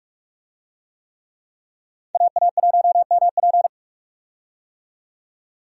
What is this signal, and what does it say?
Morse code, heard as a keyed tone, about 46 wpm, 700 Hz. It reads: AA1MP
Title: USA Call Signs - 5 Characters - 45wpm